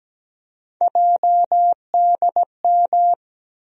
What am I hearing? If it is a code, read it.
Morse code, heard as a keyed tone, about 17 wpm, 700 Hz. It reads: JDM
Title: USA Call Sign Postfixes - 3 Letters 17wpm